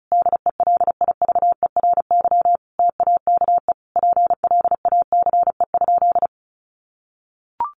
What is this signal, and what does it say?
Morse code, heard as a keyed tone, about 34 wpm, 700 Hz. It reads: DELIVERY TAKE PLACE?
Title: Sentences from Top 1500 Words 35wpm